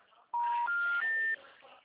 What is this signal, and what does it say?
Special information tone.